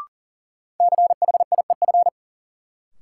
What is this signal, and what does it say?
Morse code, heard as a keyed tone, about 40 wpm, 700 Hz. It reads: CHIEF
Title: Sets of 5 Words Encore - New Word Drill - 40wpm